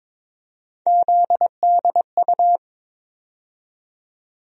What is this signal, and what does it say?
Morse code, heard as a keyed tone, about 22 wpm, 700 Hz. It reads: ZDU
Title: USA Call Sign Postfixes - 3 Letters 22wpm